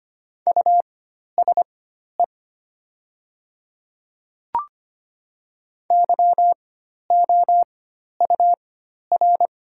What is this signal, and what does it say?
Morse code, heard as a keyed tone, about 25 wpm, spaced as for 12 wpm, 700 Hz. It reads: USE YOUR
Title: 4x Character Spacing - Top 100 Words - 25wpm